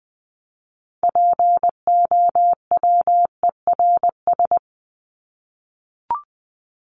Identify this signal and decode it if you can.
Morse code, heard as a keyed tone, about 20 wpm, 700 Hz. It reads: POWERS
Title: ES:S Suffix - Top 100 Words with Base Words 20wpm